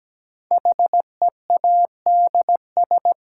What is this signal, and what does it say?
Morse code, heard as a keyed tone, about 17 wpm, 700 Hz. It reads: HEADS
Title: Sentences from Top 1000 Words - Review Part 3 - New Word Drill - 17wpm